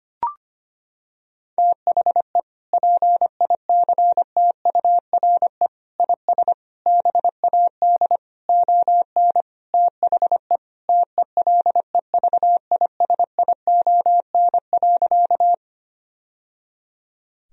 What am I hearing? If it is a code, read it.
Morse code, heard as a keyed tone, about 25 wpm, 700 Hz. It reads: THE PICTURE IS BAD ON THE TELEVISION.
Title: Sentences from Top 1000 Words - Review Part 4 - 25wpm